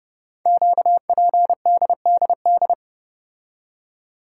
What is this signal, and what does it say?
Morse code, heard as a keyed tone, about 30 wpm, 700 Hz. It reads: QPDDD
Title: Code Groups - 5 Letters - 30wpm